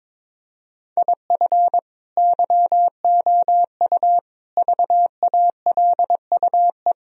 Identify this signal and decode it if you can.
Morse code, heard as a keyed tone, about 22 wpm, 700 Hz. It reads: IF YOU VALUE
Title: Sentences from Top 1000 Words - Review Part 3 - 22wpm